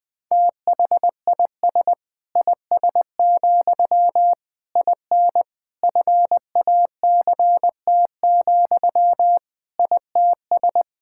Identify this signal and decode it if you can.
Morse code, heard as a keyed tone, about 20 wpm, 700 Hz. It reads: THIS IS, IN FACT, ITS
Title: Sentences from Top 1000 Words - Review Part 5 - 20wpm